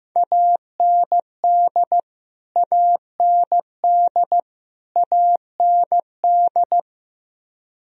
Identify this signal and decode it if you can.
Morse code, heard as a keyed tone, about 15 wpm, 700 Hz. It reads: AND AND AND
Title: Top 100 Words Repeated 15wpm